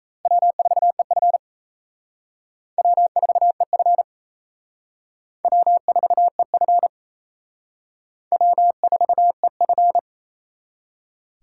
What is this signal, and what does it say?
Morse code, heard as a keyed tone, about 34 wpm, 700 Hz. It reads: W4EF W4EF W4EF W4EF
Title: Speed Racing - USA 1x2 Call Signs 28wpm